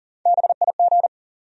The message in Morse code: BIG

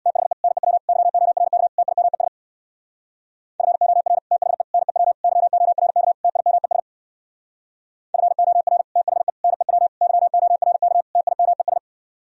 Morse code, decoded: THE NEW YORK TIMES FOR THE NEW YORK TIMES FOR THE NEW YORK TIMES

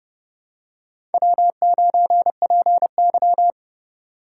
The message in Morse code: W9PY

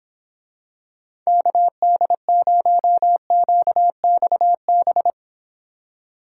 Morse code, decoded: KD0QXB

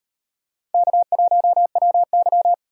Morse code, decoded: K1WY

